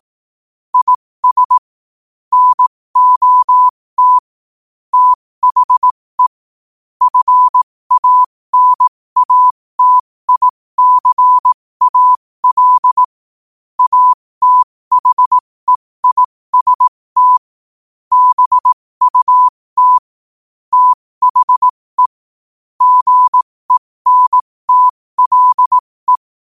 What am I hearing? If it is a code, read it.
Morse code, heard as a keyed tone, about 18 wpm, spaced as for 15 wpm, 1000 Hz. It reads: IS NOT THE FANATICAL ATHEIST BUT THE GENTLE
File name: qotdfiles / 15wpm